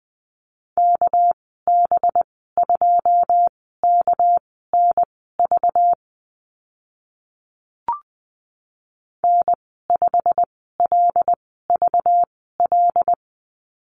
Morse code, heard as a keyed tone, about 20 wpm, spaced as for 15 wpm, 700 Hz. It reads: KB2KNV N5LVL
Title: 207 Lesson 33g - Letter j - Call Signs 20wpm